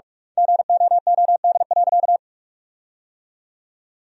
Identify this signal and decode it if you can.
Morse code, heard as a keyed tone, about 42 wpm, 700 Hz. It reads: GOOD.
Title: 2x Word-Spacing - Sentences from Top 200 Words - 45wpm